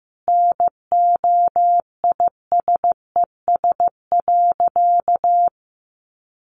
Morse code, heard as a keyed tone, about 15 wpm, 700 Hz. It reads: NOISES.
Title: Additional Easy Phrases 15wpm